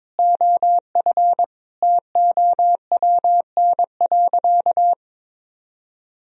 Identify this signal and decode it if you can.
Morse code, heard as a keyed tone, about 22 wpm, 700 Hz. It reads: OF TOWN.
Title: Sentences from Top 1000 Words - Review Part 2 - 22wpm